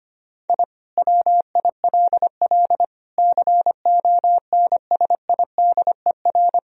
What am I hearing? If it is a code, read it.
Morse code, heard as a keyed tone, about 25 wpm, 700 Hz. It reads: I WILL CONSIDER
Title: Sentences from Top 1000 Words - Review Part 4 - 25wpm